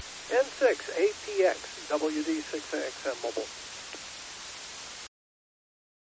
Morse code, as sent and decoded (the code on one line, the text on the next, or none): none